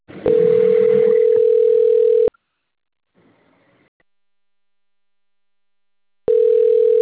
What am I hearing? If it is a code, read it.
Ringback tone (North America), 440 + 480 Hz.
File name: out- - - - - .73746